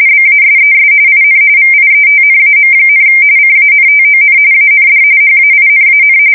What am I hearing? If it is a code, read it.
RTTY, about 45.45 baud. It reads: BASES IN VIETNAM, CANTON AND HAINAN
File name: fsk170 / battleshipnj